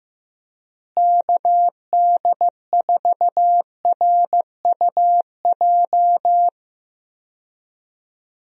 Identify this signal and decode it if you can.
Morse code, heard as a keyed tone, about 16 wpm, 700 Hz. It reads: KD4RUJ
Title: USA Call Signs - 6 Characters - Repeated 2x - 15wpm